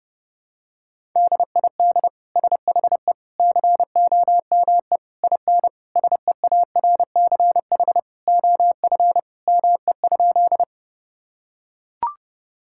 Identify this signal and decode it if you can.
Morse code, heard as a keyed tone, about 30 wpm, 700 Hz. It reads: DID SHE COME IN SEARCH OF ME?
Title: Sentences from Top 1000 Words - Review Part 5 - 30wpm